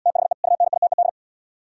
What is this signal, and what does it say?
Morse code, heard as a keyed tone, about 71 wpm, 700 Hz. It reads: THE UNITED